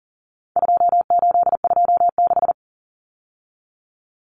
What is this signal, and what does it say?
Morse code, heard as a keyed tone, about 40 wpm, 700 Hz. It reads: 2826
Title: Numbers - 4-Digit 40wpm